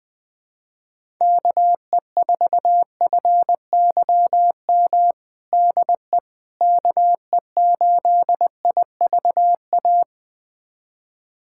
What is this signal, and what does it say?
Morse code, heard as a keyed tone, about 20 wpm, 700 Hz. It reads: KE4FYM DE KE8IVA
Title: USA Callsign DE Callsign - 6 Characters - 20wpm